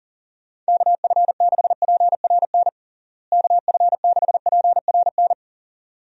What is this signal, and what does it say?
Morse code, heard as a keyed tone, about 40 wpm, 700 Hz. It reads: KF6PRN KF6PRN
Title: USA Call Signs - 6 Characters - Repeated 2x - 40wpm